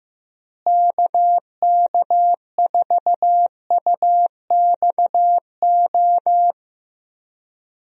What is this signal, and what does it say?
Morse code, heard as a keyed tone, about 15 wpm, 700 Hz. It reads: KK4UXO
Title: USA Call Signs - 6 Characters - Repeated 3x - 15wpm